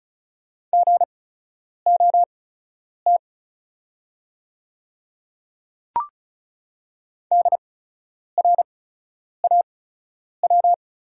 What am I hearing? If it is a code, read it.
Morse code, heard as a keyed tone, about 36 wpm, spaced as for 10 wpm, 700 Hz. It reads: GOT DRAW
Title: 8x Character Spacing - Top 300 Words - 35wpm